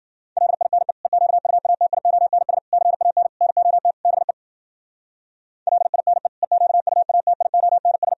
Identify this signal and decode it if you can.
Morse code, heard as a keyed tone, about 52 wpm, 700 Hz. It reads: LINE EQUATIONS CAN NOT BE LINE EQUATIONS